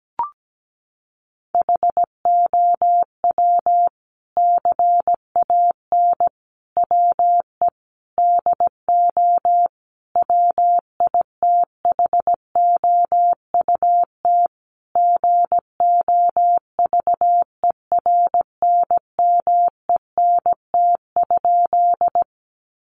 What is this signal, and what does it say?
Morse code, heard as a keyed tone, about 17 wpm, 700 Hz. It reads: HOW CAN WE DO WITHOUT GOVERNMENT?
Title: Sentences from Top 600 Words 17wpm